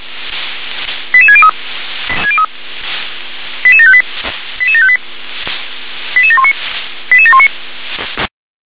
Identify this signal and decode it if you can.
Five-tone selective call (ZVEI), heard as quick runs of five tones, tones of about 70 ms. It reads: 88603 88758 88758 88719 88719